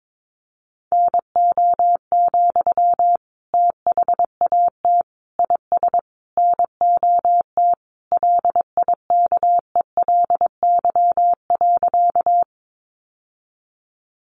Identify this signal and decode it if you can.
Morse code, heard as a keyed tone, about 22 wpm, 700 Hz. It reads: NO, THAT IS NOT LIKELY.
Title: Sentences from Top 600 Words 22wpm